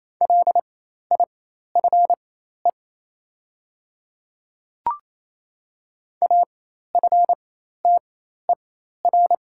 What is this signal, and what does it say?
Morse code, heard as a keyed tone, about 28 wpm, spaced as for 13 wpm, 700 Hz. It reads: LIFE AFTER
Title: 4x Character Spacing - Top 300 Words - 28wpm